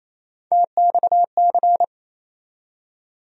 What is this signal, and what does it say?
Morse code, heard as a keyed tone, about 28 wpm, 700 Hz. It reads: TXC
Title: USA Call Sign Postfixes - 3 Letters 28wpm